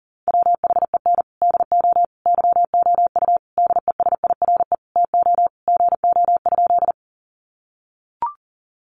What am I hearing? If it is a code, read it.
Morse code, heard as a keyed tone, about 40 wpm, 700 Hz. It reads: WHEN DO YOU DESIRE TO GO?
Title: Sentences from Top 1100 Words 40wpm